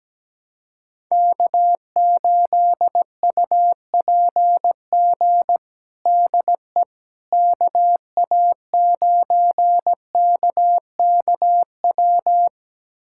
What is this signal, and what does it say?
Morse code, heard as a keyed tone, about 17 wpm, 700 Hz. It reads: K8UPG DE KA9KKW
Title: USA Callsign DE Callsign 17wpm